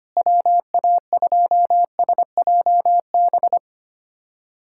Morse code, heard as a keyed tone, about 25 wpm, 700 Hz. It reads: WA2SJB
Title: USA Callsign DE Callsign 25wpm